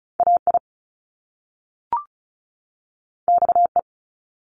Morse code, heard as a keyed tone, about 36 wpm, 700 Hz. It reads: AI XE
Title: Top Two Letter Combinations in Words 35wpm